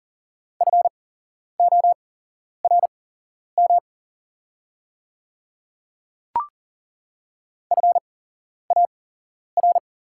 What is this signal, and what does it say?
Morse code, heard as a keyed tone, about 40 wpm, spaced as for 11 wpm, 700 Hz. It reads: FORM FAR
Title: 8x Character Spacing - Top 500 Words - 40wpm